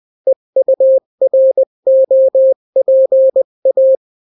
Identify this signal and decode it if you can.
Morse code, heard as a keyed tone, about 20 wpm, 530 Hz. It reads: EUROPA
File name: daily / 18wpm